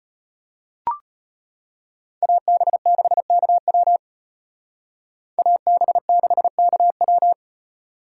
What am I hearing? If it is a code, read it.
Morse code, heard as a keyed tone, about 31 wpm, 700 Hz. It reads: AB6KW AB6KW